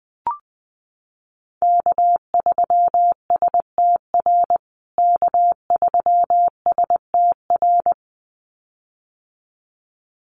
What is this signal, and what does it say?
Morse code, heard as a keyed tone, about 20 wpm, 700 Hz. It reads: K3STR K3STR
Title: USA Call Signs Repeated 2x 20wpm